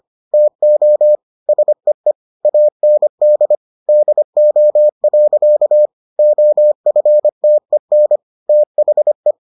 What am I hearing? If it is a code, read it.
Morse code, heard as a keyed tone, about 25 wpm, 600 Hz. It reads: TO SEE AND DO. OFTEN THE